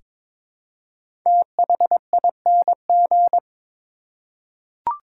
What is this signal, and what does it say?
Morse code, heard as a keyed tone, about 22 wpm, 700 Hz. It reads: THING
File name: 5 Letter Words - First 3 Letters Given - 22wpm